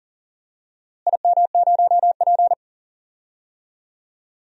Morse code, heard as a keyed tone, about 37 wpm, 700 Hz. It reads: IM0P
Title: DXCC-100 Call Signs - 4 Characters - Repeated 3x - 40wpm